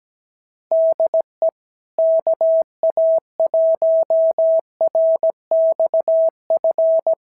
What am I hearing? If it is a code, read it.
Morse code, heard as a keyed tone, about 17 wpm, 650 Hz. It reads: DE KA1RXF
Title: SOTA QSO Practice 17wpm